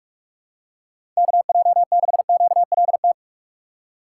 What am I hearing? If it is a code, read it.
Morse code, heard as a keyed tone, about 46 wpm, 700 Hz. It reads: KJ6QLT
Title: USA Call Signs - 6 Characters - 6x Character Spacing - 45wpm